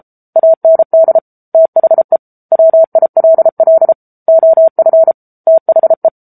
Morse code, heard as a keyed tone, about 34 wpm, 650 Hz. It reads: AND THE WILL OF THE
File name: Book Two / 07-Chapter 7 - 33 WPM